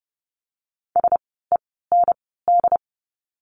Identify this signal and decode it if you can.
Morse code, heard as a keyed tone, about 30 wpm, 700 Hz. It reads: S E N D
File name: Lesson 16f - Letter c - Words 30wpm